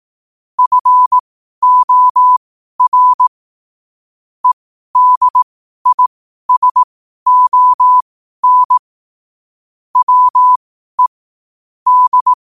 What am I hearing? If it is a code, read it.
Morse code, heard as a keyed tone, about 18 wpm, spaced as for 12 wpm, 1000 Hz. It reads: FOR EDISON WE D
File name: 12wpm